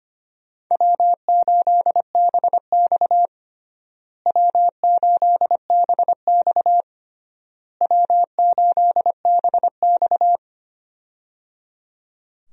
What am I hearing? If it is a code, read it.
Morse code, heard as a keyed tone, about 25 wpm, 700 Hz. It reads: W8BX W8BX W8BX
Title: USA Call Signs - 4 Characters - Repeated 3x - 25wpm